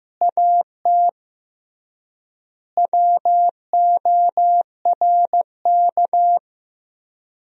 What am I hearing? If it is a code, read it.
Morse code, heard as a keyed tone, about 15 wpm, 700 Hz. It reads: AT WORK
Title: 3x Word-Spacing - Sentences from Top 200 Words - 15wpm